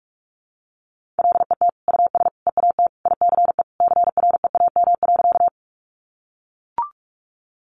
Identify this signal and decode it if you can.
Morse code, heard as a keyed tone, about 44 wpm, 700 Hz. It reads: LET US EAT ICE CREAM.
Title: Sentences from Top 1500 Words 45wpm